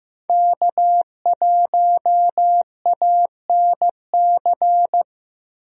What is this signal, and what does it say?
Morse code, heard as a keyed tone, about 15 wpm, 700 Hz. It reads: K1ANC
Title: USA Call Signs - 5 Characters - Repeated 2x - 15wpm